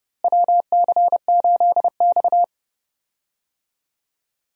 Morse code, heard as a keyed tone, about 30 wpm, 700 Hz. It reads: WC8X
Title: USA Call Signs - 4 Characters - Repeated 3x - 30wpm